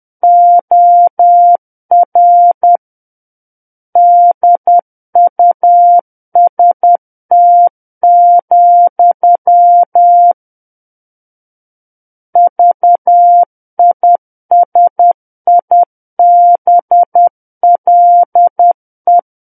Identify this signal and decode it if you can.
Morse code, heard as a keyed tone, about 10 wpm, 700 Hz. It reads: OR DUST, VISIBLE
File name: Book One / 01-Chapter 1 - 10 WPM